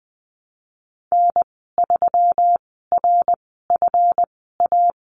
Morse code, heard as a keyed tone, about 20 wpm, spaced as for 15 wpm, 700 Hz. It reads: N3RFA
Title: 167 Lesson 27g - Number 7 - Call Signs 20wpm